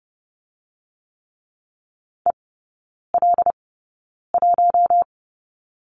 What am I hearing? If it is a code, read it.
Morse code, heard as a keyed tone, about 30 wpm, spaced as for 9 wpm, 700 Hz. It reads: EL1
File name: Lesson 18e - Letter m - Three Characters 30wpm